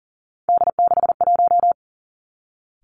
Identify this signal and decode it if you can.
Morse code, heard as a keyed tone, about 42 wpm, 700 Hz. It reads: D61
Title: Three Challenging Letter-Number Combinations 40wpm